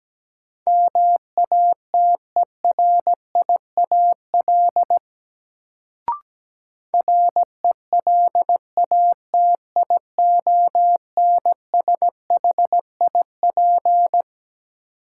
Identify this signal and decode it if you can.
Morse code, heard as a keyed tone, about 17 wpm, 700 Hz. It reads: MATERIAL RELATIONSHIP
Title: Sentences from Top 1100 Words - New Word Drill Alt - 17wpm